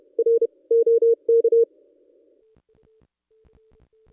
Morse code, heard as a keyed tone, about 32 wpm, 460 Hz. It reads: R OK
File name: contests / ss-cw-k5tr-op-n5rz -1600